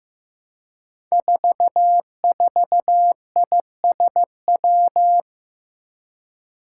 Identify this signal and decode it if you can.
Morse code, heard as a keyed tone, about 15 wpm, 700 Hz. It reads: 44ISW
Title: Five Challenging Letter-Number Combinations 15wpm